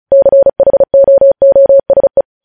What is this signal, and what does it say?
Morse code, heard as a keyed tone, about 36 wpm, 550 Hz. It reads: CHOOSE
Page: News headlines in Morse code at 35 wpm